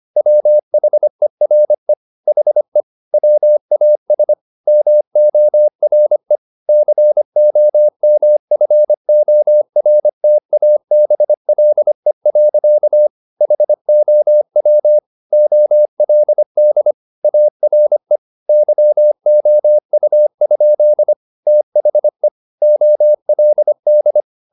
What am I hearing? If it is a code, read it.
Morse code, heard as a keyed tone, about 25 wpm, 600 Hz. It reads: WHERE HE WAS MORE COMFORTABLE. HOW OLD ARE YOU? THE OLD
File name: the old man and the sea part 1 25wpm